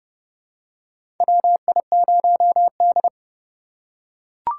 Morse code, heard as a keyed tone, about 31 wpm, 700 Hz. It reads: WI0D
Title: USA Call Signs - 4 Characters - 3x Character Spacing - 30wpm